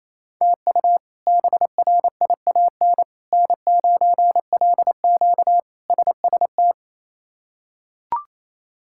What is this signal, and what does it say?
Morse code, heard as a keyed tone, about 28 wpm, 700 Hz. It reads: TU BRIAN N9LQ SST
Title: K1USN SST QSO Practice - 28wpm